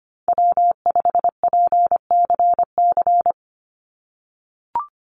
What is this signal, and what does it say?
Morse code, heard as a keyed tone, about 25 wpm, 700 Hz. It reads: W5PCC
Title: USA Call Signs 25wpm